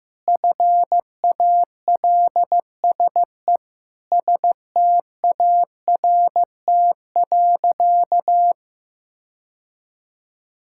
Morse code, heard as a keyed tone, about 15 wpm, 700 Hz. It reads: FALSE START.
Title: Sentences from Top 1600 Words 15wpm